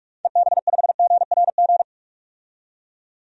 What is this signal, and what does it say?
Morse code, heard as a keyed tone, about 42 wpm, 700 Hz. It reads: EB5GRG
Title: DXCC-100 Call Signs - 6 Characters - 45wpm